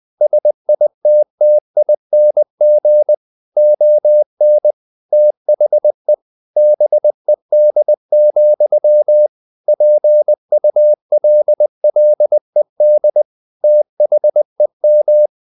Morse code, heard as a keyed tone, about 20 wpm, 600 Hz. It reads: SITTING ON THE BED, PULLED THEM
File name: the old man and the sea / the old man and the sea part 1 20wpm